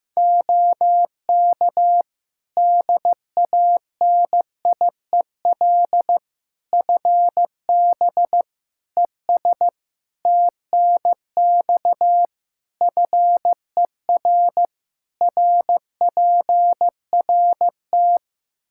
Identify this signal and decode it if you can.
Morse code, heard as a keyed tone, about 15 wpm, 700 Hz. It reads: OK DANIEL FB ES TNX FER RPRT
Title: Practice QSO - Exchange 2 - Info - 15wpm